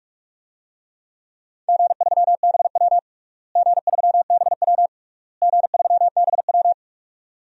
Morse code, heard as a keyed tone, about 47 wpm, 700 Hz. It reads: G3BW G3BW G3BW
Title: DXCC-100 Call Signs - 4 Characters - Repeated 3x - 45wpm